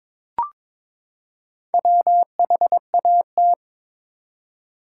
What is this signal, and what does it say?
Morse code, heard as a keyed tone, about 21 wpm, 700 Hz. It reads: WHAT
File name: Build A Phrase - 2-Word Phrases - 22wpm